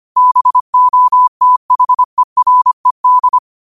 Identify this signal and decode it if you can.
Morse code, heard as a keyed tone, about 25 wpm, 1000 Hz. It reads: BOTHERED